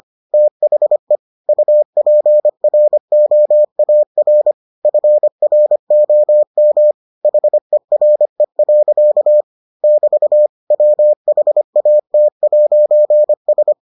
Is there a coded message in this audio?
Yes: Morse code from 0.3 to 13.7 s, heard as a keyed tone, about 25 wpm, 600 Hz. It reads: THE UPROAR FROM HERE. <BT> WHAT'S